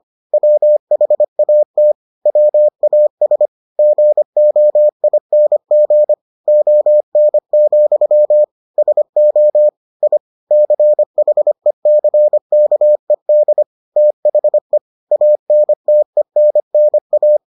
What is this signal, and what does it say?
Morse code, heard as a keyed tone, about 25 wpm, 600 Hz. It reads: WHAT WAS GOING ON, SO I CHECKED THE ANTENNA